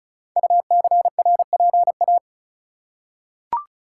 Morse code, heard as a keyed tone, about 36 wpm, 700 Hz. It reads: UCRPA
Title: Code Groups - 5 Letter:Numbers - 35wpm